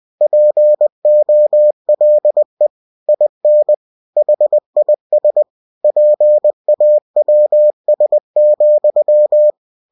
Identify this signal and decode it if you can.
Morse code, heard as a keyed tone, about 20 wpm, 600 Hz. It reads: POLE IN HIS PAWS,
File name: winnie-20wpm